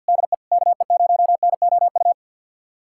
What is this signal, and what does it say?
Morse code, heard as a keyed tone, about 47 wpm, 700 Hz. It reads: DE KE0NOU
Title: USA Callsign DE Callsign - 6 Characters - 50wpm